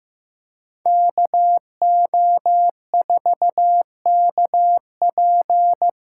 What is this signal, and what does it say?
Morse code, heard as a keyed tone, about 18 wpm, 700 Hz. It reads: KO4KP
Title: Speed Racing - USA 2x2 Call Signs 15wpm